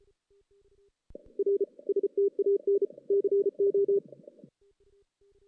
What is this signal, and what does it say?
Morse code, heard as a keyed tone, about 34 wpm, 410 Hz. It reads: R STAN CO
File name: 2016 JAN CW NAQP